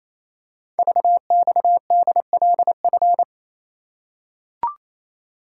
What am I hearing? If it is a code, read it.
Morse code, heard as a keyed tone, about 29 wpm, 700 Hz. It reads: VXDLF
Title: Code Groups - 5 Letters - 28wpm